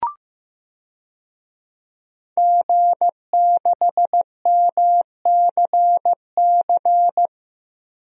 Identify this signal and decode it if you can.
Morse code, heard as a keyed tone, about 16 wpm, 700 Hz. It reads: G6MCC
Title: DXCC-100 Call Signs - 5 Characters - Repeated 2x - 15wpm